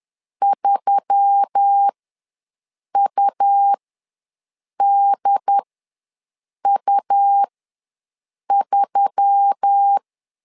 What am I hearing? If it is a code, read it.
Morse code, heard as a keyed tone, about 10 wpm, spaced as for 6 wpm, 800 Hz. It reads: 3UDU3
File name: LEZIONI / 4 Q1S3 ws 10wpm 2437